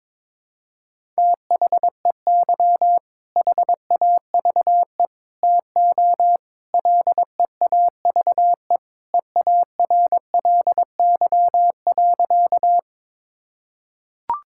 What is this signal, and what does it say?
Morse code, heard as a keyed tone, about 22 wpm, 700 Hz. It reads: THEY HAVE TO LEAVE EARLY.
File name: practice / 22wpm / Sentences with 3-Word Phrases - 22wpm